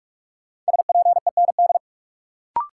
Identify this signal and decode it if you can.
Morse code, heard as a keyed tone, about 46 wpm, 700 Hz. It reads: SPEND
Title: Sentences from Top 800 Words - New Word Drill - 45wpm